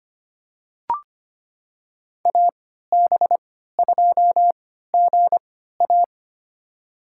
Morse code, heard as a keyed tone, about 25 wpm, spaced as for 14 wpm, 700 Hz. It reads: AB2GA
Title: USA Call Signs - 5 Characters - 3x Character Spacing - 25wpm